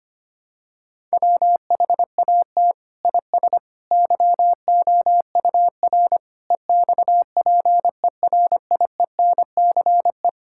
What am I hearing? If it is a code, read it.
Morse code, heard as a keyed tone, about 25 wpm, 700 Hz. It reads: WHAT IS YOUR EXPERIENCE
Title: Sentences from Top 1000 Words - Review Part 5 - 25wpm